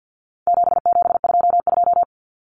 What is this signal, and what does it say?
Morse code, heard as a keyed tone, about 50 wpm, 700 Hz. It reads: 7722
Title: Numbers - 4-Digit 50wpm